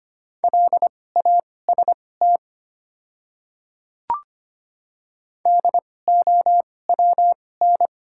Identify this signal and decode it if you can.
Morse code, heard as a keyed tone, about 25 wpm, spaced as for 18 wpm, 700 Hz. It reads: LAST DOWN
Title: 2x Character Spacing - Top 500 Words - 25wpm